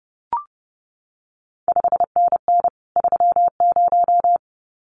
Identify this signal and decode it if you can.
Morse code, heard as a keyed tone, about 30 wpm, 700 Hz. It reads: 5NN 30
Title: CQ WW DX Contest Exchange Encore - 30wpm